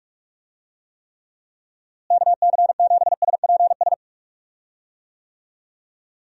Morse code, heard as a keyed tone, about 46 wpm, 700 Hz. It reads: KC7SPS
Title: USA Call Signs - 6 Characters - Repeated 3x - 45wpm